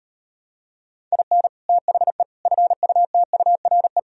Morse code, heard as a keyed tone, about 38 wpm, 700 Hz. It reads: IN THE FUTURE